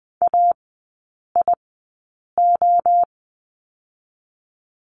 Morse code, heard as a keyed tone, about 20 wpm, spaced as for 8 wpm, 700 Hz. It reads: AIO
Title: 041 Lesson 8e - Number 1 - Three Characters 20wpm